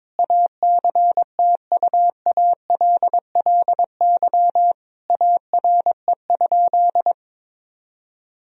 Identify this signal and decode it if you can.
Morse code, heard as a keyed tone, about 22 wpm, 700 Hz. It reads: ACTUALLY ARE?
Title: Sentences from Top 1000 Words - Review Part 4 - 22wpm